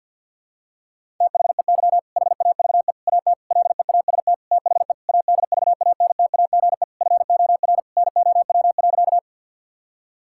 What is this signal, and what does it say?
Morse code, heard as a keyed tone, about 47 wpm, 700 Hz. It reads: THEY HAVE AT LEAST THE ADVANTAGE FOR NOW.
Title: Sentences from Top 1000 Words - Review Part 6 - 50wpm